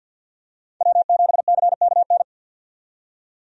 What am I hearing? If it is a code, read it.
Morse code, heard as a keyed tone, about 48 wpm, 700 Hz. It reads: W7CKN
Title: USA Call Signs - 5 Characters - 50wpm